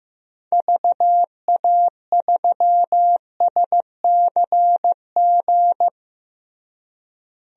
Morse code, heard as a keyed tone, about 15 wpm, 700 Hz. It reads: VA3SCG
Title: DXCC-100 Call Signs - 6 Characters - Repeated 2x - 15wpm